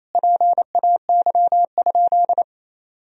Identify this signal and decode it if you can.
Morse code, heard as a keyed tone, about 28 wpm, 700 Hz. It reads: PAY?
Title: Sentences from Top 1000 Words - Review Part 4 - 28wpm